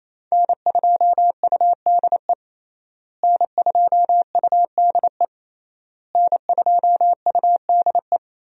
Morse code, heard as a keyed tone, about 27 wpm, 700 Hz. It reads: N2UDE N2UDE N2UDE
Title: USA Call Signs - 5 Characters - Repeated 3x - 28wpm